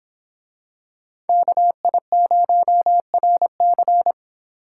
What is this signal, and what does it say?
Morse code, heard as a keyed tone, about 25 wpm, 700 Hz. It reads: KI0RC